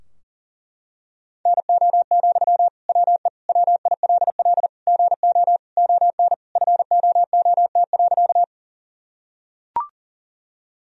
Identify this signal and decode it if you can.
Morse code, heard as a keyed tone, about 40 wpm, 700 Hz. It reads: NO, WE WILL GO ON FOOT.
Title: Sentences from Top 1000 Words - Review Part 3 - 40wpm